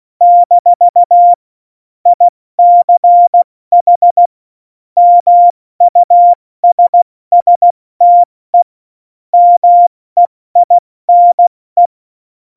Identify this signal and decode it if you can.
Morse code, heard as a keyed tone, about 16 wpm, 700 Hz. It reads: - ICH MUSSTE MEINE
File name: Morsekurs / Witze / Witz 13. August 2019, 70 F80